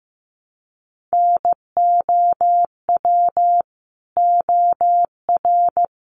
Morse code, heard as a keyed tone, about 15 wpm, 700 Hz. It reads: NOW OR
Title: Sentences from Top 300 Words 15wpm